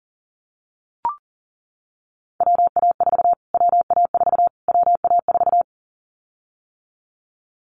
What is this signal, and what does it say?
Morse code, heard as a keyed tone, about 40 wpm, 700 Hz. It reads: WA4 WA4 WA4
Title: Callsign Prefixes for Top 25 DXCC - Repeated 3x - 40wpm